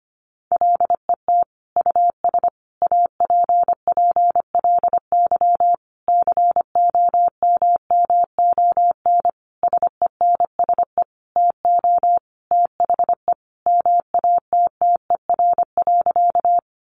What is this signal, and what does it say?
Morse code, heard as a keyed tone, about 25 wpm, 700 Hz. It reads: LET US APPLY COMMON SENSE TO THE MATTER.
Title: Sentences from Top 1200 Words 25wpm